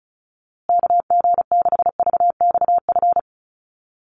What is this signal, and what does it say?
Morse code, heard as a keyed tone, about 34 wpm, 700 Hz. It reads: KG6VXF
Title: USA Call Signs Repeated 2x 35wpm